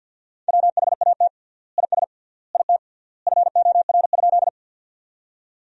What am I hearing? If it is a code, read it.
Morse code, heard as a keyed tone, about 50 wpm, 700 Hz. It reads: WHAT IS IT FOR?
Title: 3x Word-Spacing - Sentences from Top 100 Words - 50wpm